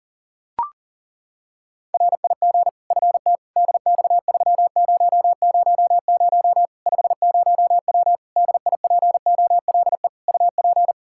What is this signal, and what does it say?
Morse code, heard as a keyed tone, about 40 wpm, 700 Hz. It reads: RIG FT DX3000 50W DIPOLE UP
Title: Practice QSO - Exchange 2 - Info - 40wpm